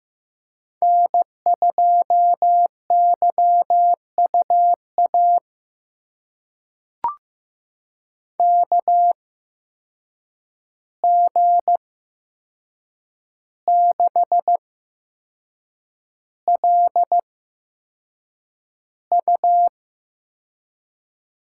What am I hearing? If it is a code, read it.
Morse code, heard as a keyed tone, about 15 wpm, spaced as for 6 wpm, 700 Hz. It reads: N2YUA KG6LU
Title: USA Call Signs - 8x Character Spacing - 15wpm